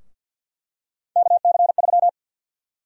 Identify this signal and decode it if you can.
Morse code, heard as a keyed tone, about 51 wpm, 700 Hz. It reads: KC3
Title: USA Call Sign Prefixes - 50wpm